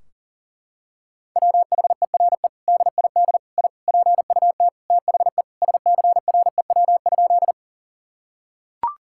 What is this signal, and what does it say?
Morse code, heard as a keyed tone, about 37 wpm, 700 Hz. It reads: WHERE DID I PUT THE SCREW?
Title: Sentences from Top 1600 Words 40wpm